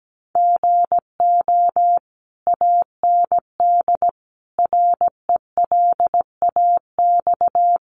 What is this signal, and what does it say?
Morse code, heard as a keyed tone, about 17 wpm, 700 Hz. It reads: GO AND RELAX
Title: Sentences from Top 1400 Words 17wpm